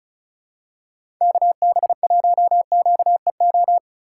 Morse code, heard as a keyed tone, about 36 wpm, 700 Hz. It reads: KB1QEO
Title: USA Call Signs - 6 Characters - 2x Character Spacing - 35wpm